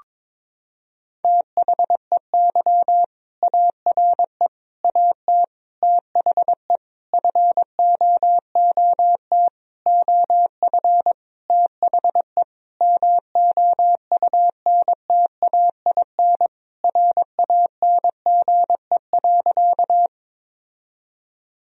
Morse code, heard as a keyed tone, about 22 wpm, 700 Hz. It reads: THEY ARE AT THE FOOT OF THE MOUNTAIN RANGE.
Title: Sentences from Top 1000 Words - Review Part 4 - 22wpm